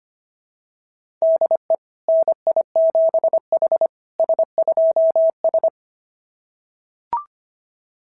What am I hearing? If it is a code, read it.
Morse code, heard as a keyed tone, about 25 wpm, 690 Hz. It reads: DE NI7H S2S
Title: SOTA QSO Practice 25wpm